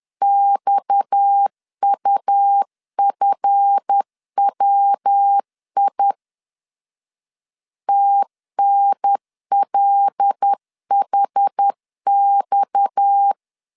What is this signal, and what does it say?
Morse code, heard as a keyed tone, about 10 wpm, 800 Hz. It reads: XUFWI TNLHX
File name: LEZIONI / 7 GF9Z / 7 GF 10wpm 2514